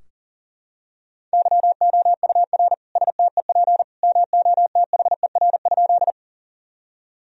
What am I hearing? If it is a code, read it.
Morse code, heard as a keyed tone, about 40 wpm, 700 Hz. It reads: YOUR STEP MOTHER?
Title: Sentences from Top 1000 Words - Review Part 2 - 40wpm